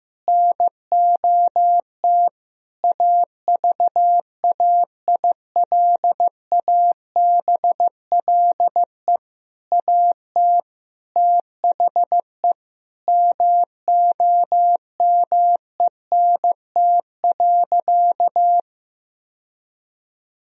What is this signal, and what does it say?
Morse code, heard as a keyed tone, about 15 wpm, 700 Hz. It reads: NOT AVAILABLE AT THE MOMENT.
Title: Sentences from Top 1000 Words - Review Part 3 - 15wpm